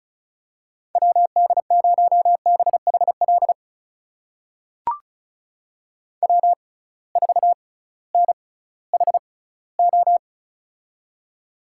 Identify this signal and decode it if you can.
Morse code, heard as a keyed tone, about 36 wpm, spaced as for 19 wpm, 700 Hz. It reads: WD0BHL W4NHO
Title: USA Call Signs - 6x Character Spacing - 35wpm